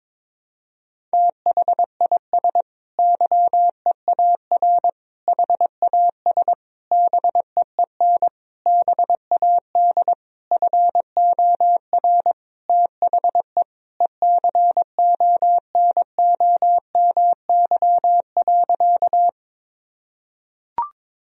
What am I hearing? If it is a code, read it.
Morse code, heard as a keyed tone, about 22 wpm, 700 Hz. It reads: THIS YEAR HAS BEEN BAD FOR THE ECONOMY.
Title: Sentences from Top 1000 Words - Review Part 3 - 22wpm